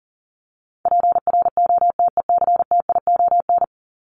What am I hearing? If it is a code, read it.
Morse code, heard as a keyed tone, about 40 wpm, 700 Hz. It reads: PROTECTION
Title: Top 1501-1750 Words 40wpm